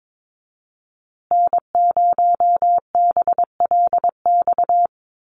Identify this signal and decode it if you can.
Morse code, heard as a keyed tone, about 22 wpm, 700 Hz. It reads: N0BLX
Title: USA Call Signs Repeated 2x 22wpm